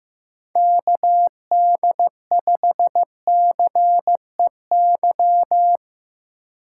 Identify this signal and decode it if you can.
Morse code, heard as a keyed tone, about 18 wpm, 700 Hz. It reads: KD5CEY
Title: Speed Racing - USA 2x3 Call Signs 15wpm